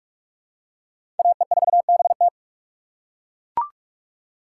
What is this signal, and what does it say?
Morse code, heard as a keyed tone, about 42 wpm, 700 Hz. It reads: AE4BT
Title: USA Call Signs - 5 Characters - 2x Character Spacing - 45wpm